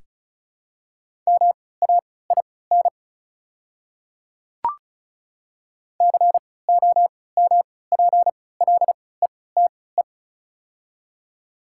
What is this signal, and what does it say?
Morse code, heard as a keyed tone, about 36 wpm, spaced as for 20 wpm, 690 Hz. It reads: MAIN COMPLETE
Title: 3x Character Spacing - Top 500 Words - 35wpm